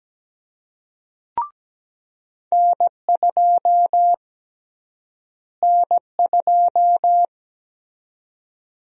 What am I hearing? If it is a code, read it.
Morse code, heard as a keyed tone, about 17 wpm, 700 Hz. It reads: N2 N2
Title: USA Call Sign Prefixes - Repeated 2x - 17wpm